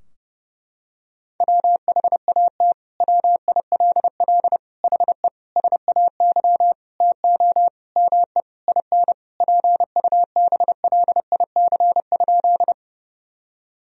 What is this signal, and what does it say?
Morse code, heard as a keyed tone, about 30 wpm, 700 Hz. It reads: WHAT WILL HE SAY TO ME IN PUBLIC?
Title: Sentences from Top 1000 Words - Review Part 3 - 30wpm